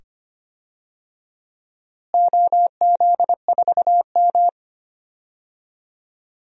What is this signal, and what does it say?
Morse code, heard as a keyed tone, about 25 wpm, 700 Hz. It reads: OZ4M
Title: DXCC-100 Call Signs - 4 Characters - Repeated 2x - 25wpm